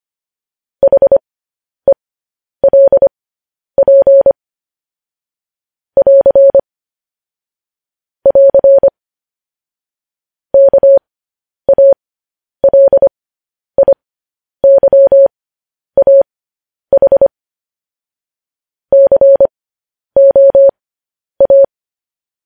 Morse code, heard as a keyed tone, about 25 wpm, spaced as for 10 wpm, 550 Hz. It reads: HELP <AR> <AR> KALIYAH COA
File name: podcast / News headlines-1x-25x10